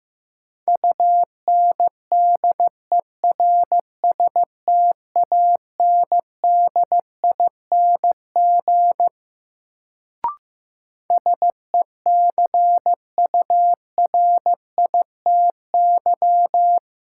Morse code, heard as a keyed tone, about 16 wpm, 700 Hz. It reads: UNDERSTANDING SECURITY
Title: Sentences from Top 1400 Words - New Word Drill Alt - 15wpm